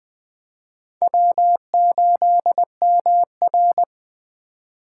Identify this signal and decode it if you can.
Morse code, heard as a keyed tone, about 20 wpm, 700 Hz. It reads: W8MR
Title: USA Call Signs - 4 Characters - 3x Character Spacing - 20wpm